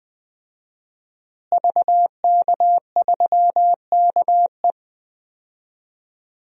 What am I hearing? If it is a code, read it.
Morse code, heard as a keyed tone, about 20 wpm, 700 Hz. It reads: VK3KE
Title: DXCC-100 Call Signs - 5 Characters - Repeated 3x - 20wpm